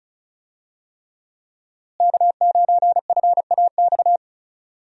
Morse code, heard as a keyed tone, about 36 wpm, 700 Hz. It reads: K9FAX
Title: USA Call Signs - 5 Characters - Repeated 3x - 35wpm